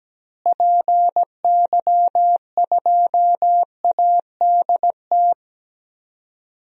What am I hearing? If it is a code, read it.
Morse code, heard as a keyed tone, about 17 wpm, 700 Hz. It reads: PY2ADT
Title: DXCC-100 Call Signs - 6 Characters - Repeated 2x - 17wpm